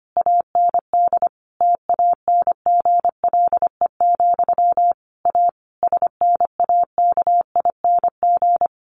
Morse code, heard as a keyed tone, about 25 wpm, 700 Hz. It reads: AND TANGLE, A SNAKING
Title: A Sound Of Thunder 25wpm